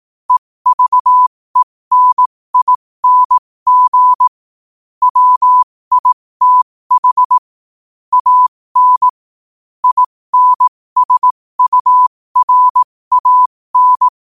Morse code, heard as a keyed tone, about 18 wpm, spaced as for 15 wpm, 1000 Hz. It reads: EVENING WITH AN INSURAN
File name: qotdfiles / 15wpm